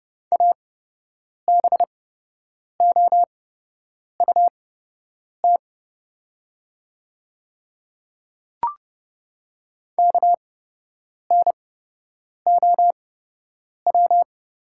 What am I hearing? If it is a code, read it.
Morse code, heard as a keyed tone, about 30 wpm, spaced as for 8 wpm, 700 Hz. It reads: ABOUT KNOW
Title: 8x Character Spacing - Top 100 Words - 30wpm